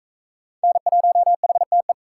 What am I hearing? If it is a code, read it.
Morse code, heard as a keyed tone, about 36 wpm, 700 Hz. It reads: N1HTE